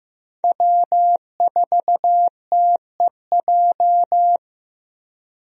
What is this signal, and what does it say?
Morse code, heard as a keyed tone, about 15 wpm, 700 Hz. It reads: W4TEJ
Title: USA Call Signs - 5 Characters - Repeated 3x - 15wpm